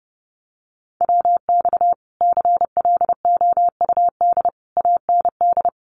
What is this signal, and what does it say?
Morse code, heard as a keyed tone, about 30 wpm, 700 Hz. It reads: WX CLOUD AND
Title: Common QSO Phrases Encore 30wpm